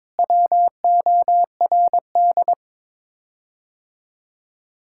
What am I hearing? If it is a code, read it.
Morse code, heard as a keyed tone, about 22 wpm, 700 Hz. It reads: WORD
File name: practice / 22wpm / Sets of 6 Words - 22wpm